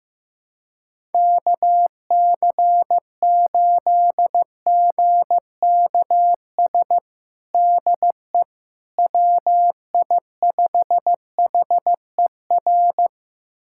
Morse code, heard as a keyed tone, about 15 wpm, 700 Hz. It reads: KC8GKS DE WI5HER
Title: USA Callsign DE Callsign 15wpm